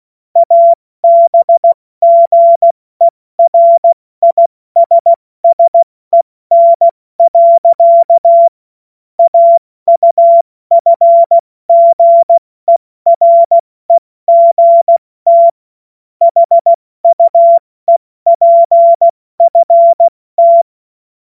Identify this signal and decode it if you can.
Morse code, heard as a keyed tone, about 16 wpm, 680 Hz. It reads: ABGERISSEN. AUFGEREGT HUEPFT
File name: Morsekurs / Witze / Witz 02. August 2022 70 F80